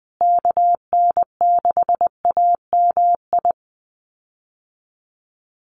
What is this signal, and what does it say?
Morse code, heard as a keyed tone, about 19 wpm, 700 Hz. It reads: KN6AMI
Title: USA Call Signs Repeated 3x 20wpm